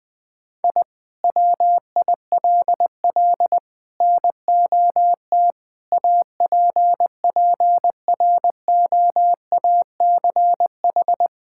Morse code, heard as a keyed tone, about 20 wpm, 700 Hz. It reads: I WILL NOT APPROACH
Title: Sentences from Top 1000 Words - Review Part 6 - 20wpm